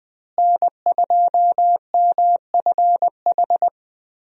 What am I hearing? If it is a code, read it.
Morse code, heard as a keyed tone, about 20 wpm, 700 Hz. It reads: N2MFH
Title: USA Call Signs - 5 Characters - Repeated 2x - 20wpm